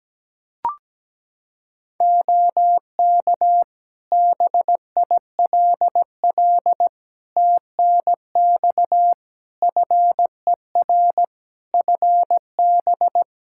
Morse code, heard as a keyed tone, about 17 wpm, 700 Hz. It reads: OK BILL TNX FER FB
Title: Practice QSO - Exchange 4 - Ending - 17wpm